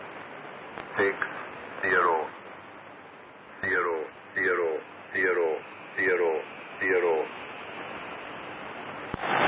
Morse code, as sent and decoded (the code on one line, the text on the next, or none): none